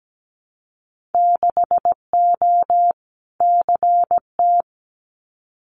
Morse code, heard as a keyed tone, about 17 wpm, 700 Hz. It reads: 6O CT
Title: Winter Field Day Exchanges Encore 17wpm